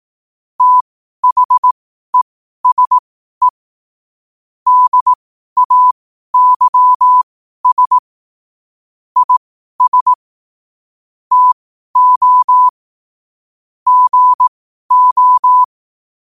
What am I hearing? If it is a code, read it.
Morse code, heard as a keyed tone, about 18 wpm, spaced as for 12 wpm, 1000 Hz. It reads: THESE DAYS IS TO GO
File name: qotdfiles / 12wpm